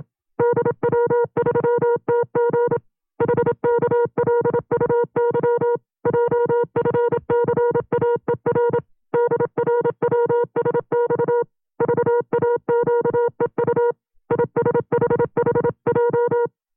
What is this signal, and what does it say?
Morse code, heard as a keyed tone, about 27 wpm, 480 Hz. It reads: DW3TG HKLUY JFCAER DRWSX VAQEU ISHHJ
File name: CW-practice/25WPM